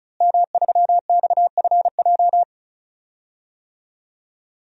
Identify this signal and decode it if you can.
Morse code, heard as a keyed tone, about 36 wpm, 700 Hz. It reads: M3XFJ
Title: DXCC-100 Call Signs - 5 Characters - Repeated 2x - 35wpm